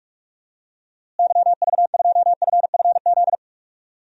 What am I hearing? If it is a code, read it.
Morse code, heard as a keyed tone, about 46 wpm, 700 Hz. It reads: YV2FFZ
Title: DXCC-100 Call Signs - 6 Characters - Repeated 3x - 45wpm